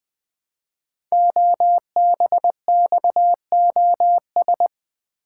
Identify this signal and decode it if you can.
Morse code, heard as a keyed tone, about 20 wpm, 700 Hz. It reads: OBXOS
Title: Code Groups - 5 Letters - 20wpm